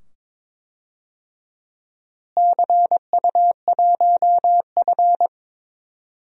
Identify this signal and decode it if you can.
Morse code, heard as a keyed tone, about 22 wpm, 700 Hz. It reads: CU1F
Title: DXCC-100 Call Signs - 4 Characters - Repeated 2x - 22wpm